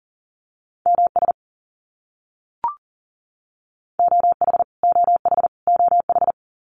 Morse code, heard as a keyed tone, about 40 wpm, 700 Hz. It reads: MS OH OH OH
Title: US States Repeated 3x 40wpm